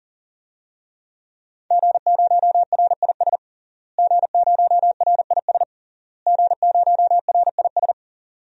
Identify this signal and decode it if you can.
Morse code, heard as a keyed tone, about 40 wpm, 700 Hz. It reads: G0RIS G0RIS G0RIS
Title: DXCC-100 Call Signs - 5 Characters - Repeated 3x - 40wpm